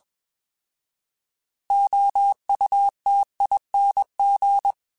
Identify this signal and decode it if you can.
Morse code, heard as a keyed tone, about 21 wpm, 800 Hz. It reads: OUTING
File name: Alphabetical / Alphabetical mp3s / Os @20